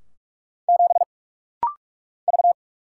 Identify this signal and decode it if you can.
Morse code, heard as a keyed tone, about 44 wpm, 700 Hz. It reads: 7 V
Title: Single Letter-Number Alt - Rapid-Fire 45wpm